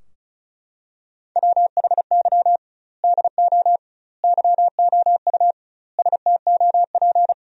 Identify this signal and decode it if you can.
Morse code, heard as a keyed tone, about 34 wpm, 700 Hz. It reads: WHY DO YOU STOP
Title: 2x Word-Spacing - Sentences from Top 300 Words - 35wpm